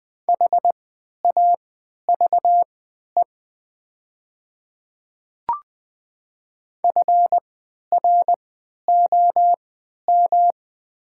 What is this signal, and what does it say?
Morse code, heard as a keyed tone, about 20 wpm, spaced as for 11 wpm, 700 Hz. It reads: HAVE FROM
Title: 3x Character Spacing - Top 100 Words - 20wpm